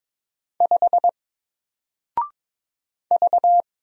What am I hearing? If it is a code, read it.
Morse code, heard as a keyed tone, about 22 wpm, 700 Hz. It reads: 5 V
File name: Single-Letter-Number-22wpm